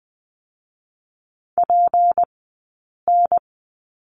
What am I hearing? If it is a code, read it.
Morse code, heard as a keyed tone, about 20 wpm, spaced as for 8 wpm, 700 Hz. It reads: PN
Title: 204 Lesson 33d - Letter j - Two Characters 20wpm